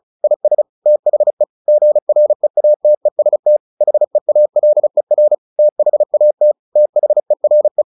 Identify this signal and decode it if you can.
Morse code, heard as a keyed tone, about 34 wpm, 600 Hz. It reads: IS THE GREATEST HEALER THAT THERE
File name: the old man and the sea 2 35wpm